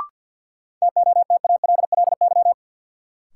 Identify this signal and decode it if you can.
Morse code, heard as a keyed tone, about 53 wpm, 700 Hz. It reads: TOTALLY
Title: Sentences from Top 1600 Words - New Word Drill Alt - 50wpm